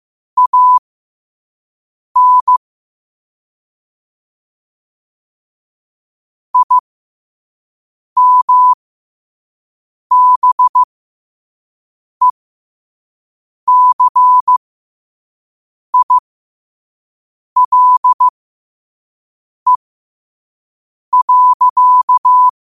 Morse code, heard as a keyed tone, about 15 wpm, spaced as for 5 wpm, 1000 Hz. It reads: AN IMBECILE.
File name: 5wpm